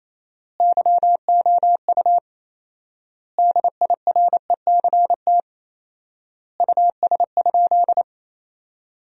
Morse code, heard as a keyed tone, about 28 wpm, 700 Hz. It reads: YOU DIRECT US?
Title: 4x Word-Spacing - Sentences from Top 400 Words - 28wpm